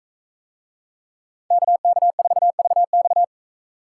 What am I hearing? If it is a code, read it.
Morse code, heard as a keyed tone, about 42 wpm, 700 Hz. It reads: KK4VX